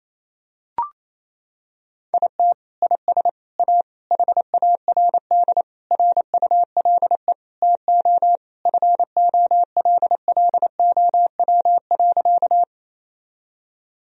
Morse code, heard as a keyed tone, about 27 wpm, 700 Hz. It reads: IT IS A HARD RULE TO FOLLOW.
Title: Sentences from Top 1000 Words - Review Part 2 - 28wpm